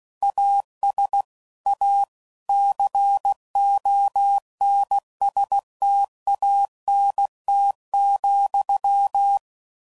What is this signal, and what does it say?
Morse code, heard as a keyed tone, about 16 wpm, 800 Hz. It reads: AS A CONSTANT,
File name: CW Code Groups @15WPM / Technical Subjects / XL